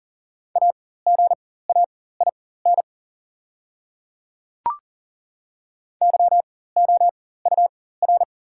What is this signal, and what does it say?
Morse code, heard as a keyed tone, about 40 wpm, spaced as for 19 wpm, 690 Hz. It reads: AGAIN YOUR
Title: 4x Character Spacing - Top 200 Words - 40wpm